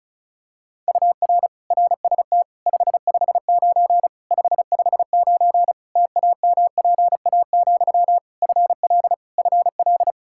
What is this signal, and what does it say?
Morse code, heard as a keyed tone, about 34 wpm, 700 Hz. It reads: UR RST 559 559 TAMPA, FL FL
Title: Practice QSO - Exchange 1 - Beginning - 35wpm